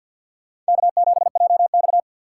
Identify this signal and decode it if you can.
Morse code, heard as a keyed tone, about 63 wpm, 700 Hz. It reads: K7JX